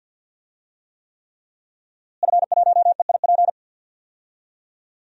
Morse code, heard as a keyed tone, about 49 wpm, 700 Hz. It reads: F1EIP
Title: DXCC-100 Call Signs - 5 Characters - Repeated 3x - 50wpm